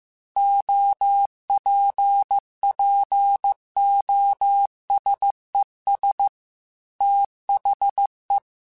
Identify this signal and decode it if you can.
Morse code, heard as a keyed tone, about 15 wpm, 800 Hz. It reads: OPPOSES THE
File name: Technical Subjects / Technical Subjects-mp3s / XL @13